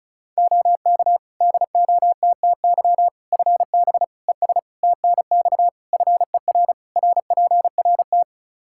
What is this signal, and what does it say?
Morse code, heard as a keyed tone, about 36 wpm, 700 Hz. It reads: OK DOTTY FB ES TNX FER RPRT
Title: K1USN SST QSO Practice 35wpm